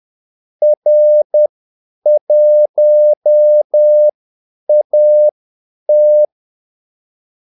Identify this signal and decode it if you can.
Morse code, heard as a keyed tone, about 10 wpm, spaced as for 8 wpm, 600 Hz. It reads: R1AT
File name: Callsigns-10-8wpm